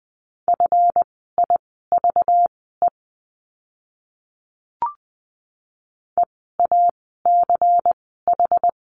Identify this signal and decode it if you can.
Morse code, heard as a keyed tone, about 20 wpm, spaced as for 15 wpm, 700 Hz. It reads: FIVE EACH
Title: 192 Lesson 31f - Letter v - Words 20wpm